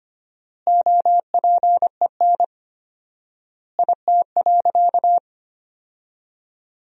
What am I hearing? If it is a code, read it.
Morse code, heard as a keyed tone, about 25 wpm, 700 Hz. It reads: OPEN IT.
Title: 4x Word-Spacing - Sentences from Top 300 Words - 25wpm